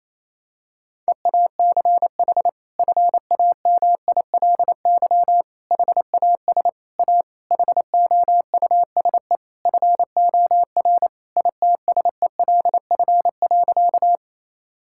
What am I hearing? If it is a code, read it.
Morse code, heard as a keyed tone, about 28 wpm, 700 Hz. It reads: EACH FAMILY HAS A HOUSE FOR ITSELF.
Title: Sentences from Top 1000 Words - Review Part 5 - 28wpm